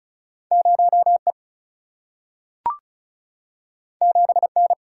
Morse code, heard as a keyed tone, about 36 wpm, 700 Hz. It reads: 0E 7N
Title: Single-Letter Single-Number 35wpm